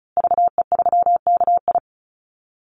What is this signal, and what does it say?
Morse code, heard as a keyed tone, about 34 wpm, 700 Hz. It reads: VE3KI
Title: Call Signs Repeated 2x - 35wpm